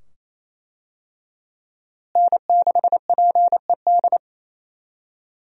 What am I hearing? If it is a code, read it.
Morse code, heard as a keyed tone, about 27 wpm, 700 Hz. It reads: N6PED
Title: USA Call Signs - 5 Characters - Repeated 2x - 28wpm